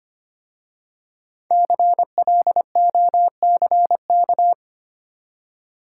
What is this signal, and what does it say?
Morse code, heard as a keyed tone, about 25 wpm, 700 Hz. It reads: CLOCK
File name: practice / 25wpm / Easy Words with Hints - 25wpm